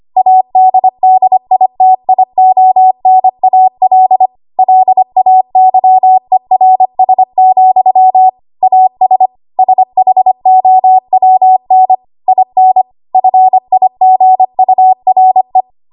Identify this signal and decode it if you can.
Morse code, heard as a keyed tone, about 25 wpm, 750 Hz. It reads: ADDITIONAL LAYERS, AS SHOWN IN FIGURE
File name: Archive / 25 WPM / 25WPM